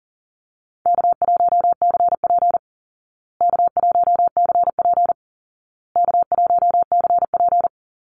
Morse code, heard as a keyed tone, about 40 wpm, 700 Hz. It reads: K1CP K1CP K1CP
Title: Easy Call Signs Repeated 3x - 40wpm